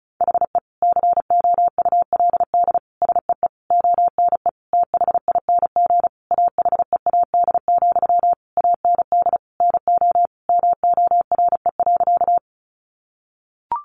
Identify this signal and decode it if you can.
Morse code, heard as a keyed tone, about 34 wpm, 700 Hz. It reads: HE COULD SEE ONE THING AHEAD, AND NO MORE.
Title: Sentences from Top 1100 Words 35wpm